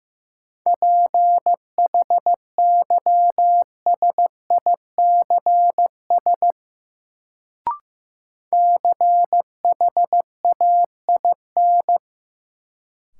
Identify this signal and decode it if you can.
Morse code, heard as a keyed tone, about 15 wpm, 700 Hz. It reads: PHYSICS CHAIN
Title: Sentences from Top 1500 Words - New Word Drill Alt - 15wpm